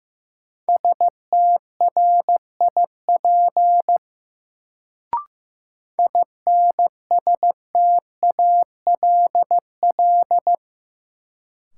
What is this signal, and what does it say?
Morse code, heard as a keyed tone, about 15 wpm, 700 Hz. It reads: STRIP INSTALL
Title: Sentences from Top 1600 Words - New Word Drill - 15wpm